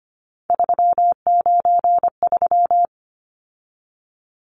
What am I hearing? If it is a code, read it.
Morse code, heard as a keyed tone, about 25 wpm, 700 Hz. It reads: 393
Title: Numbers - 3-Digit 25wpm